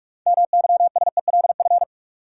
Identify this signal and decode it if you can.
Morse code, heard as a keyed tone, about 44 wpm, 700 Hz. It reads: MYSELF
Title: Sentences from Top 1300 Words - New Word Drill - 45wpm